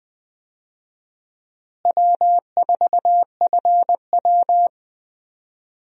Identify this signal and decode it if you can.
Morse code, heard as a keyed tone, about 20 wpm, 700 Hz. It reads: W4FW
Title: USA Call Signs - 4 Characters - Repeated 2x - 20wpm